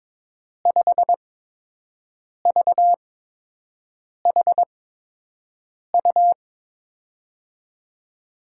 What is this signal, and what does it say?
Morse code, heard as a keyed tone, about 22 wpm, spaced as for 6 wpm, 700 Hz. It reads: 5VHU
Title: USA Call Signs - 5 Characters - 8x Character Spacing - 22wpm